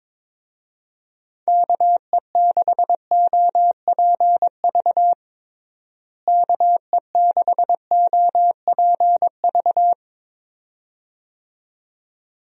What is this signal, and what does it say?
Morse code, heard as a keyed tone, about 22 wpm, 700 Hz. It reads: KE6OPV KE6OPV
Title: USA Call Signs - 6 Characters - Repeated 2x - 22wpm